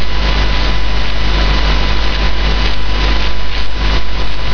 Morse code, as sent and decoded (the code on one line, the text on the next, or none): none